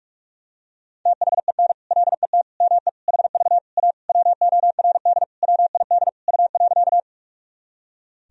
Morse code, heard as a keyed tone, about 46 wpm, 700 Hz. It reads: THEN LET ME HV A WORD WID U.
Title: Sentences from Top 300 Words and Abbreviations 45wpm